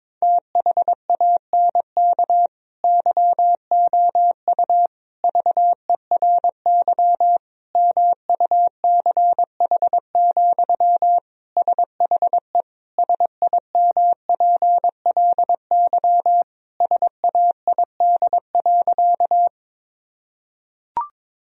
Morse code, heard as a keyed tone, about 22 wpm, 700 Hz. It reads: THANK YOU VERY MUCH, SHE SIMPLY SAID.
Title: Sentences from Top 1000 Words - Review Part 4 - 22wpm